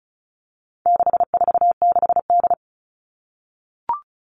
Morse code, heard as a keyed tone, about 34 wpm, 700 Hz. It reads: 646D
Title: All Cut Numbers - 4-Digit 35wpm